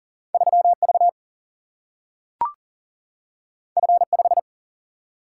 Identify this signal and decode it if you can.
Morse code, heard as a keyed tone, about 40 wpm, 700 Hz. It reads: 3 V F 5
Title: Single-Letter Single-Number 40wpm